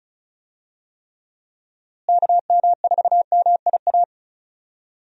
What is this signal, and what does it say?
Morse code, heard as a keyed tone, about 34 wpm, 700 Hz. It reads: KM4MIA
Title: USA Call Signs - 6 Characters - Repeated 3x - 35wpm